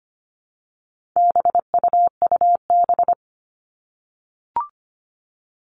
Morse code, heard as a keyed tone, about 25 wpm, 700 Hz. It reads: BUUB
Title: All Cut Numbers - 4-Digit 25wpm